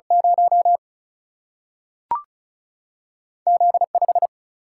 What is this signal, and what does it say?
Morse code, heard as a keyed tone, about 36 wpm, 700 Hz. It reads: 0 Z5
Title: Single-Letter Single-Number 35wpm